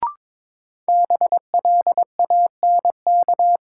Morse code, heard as a keyed tone, about 22 wpm, 700 Hz. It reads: BLANK
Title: Sentences from Top 1500 Words - New Word Drill - 22wpm